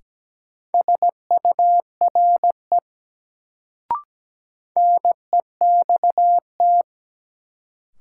Morse code, heard as a keyed tone, about 17 wpm, 700 Hz. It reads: SURE NEXT
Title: Sentences from Top 300 Words - New Word Drill - 17wpm